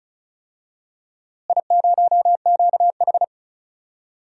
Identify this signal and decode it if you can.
Morse code, heard as a keyed tone, about 36 wpm, 700 Hz. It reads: I0QH
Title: DXCC-100 Call Signs - 4 Characters - Repeated 3x - 35wpm